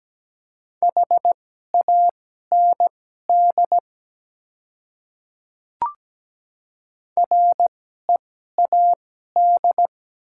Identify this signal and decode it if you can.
Morse code, heard as a keyed tone, about 17 wpm, spaced as for 12 wpm, 700 Hz. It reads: HAND READ
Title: 2x Character Spacing - Top 200 Words - 17wpm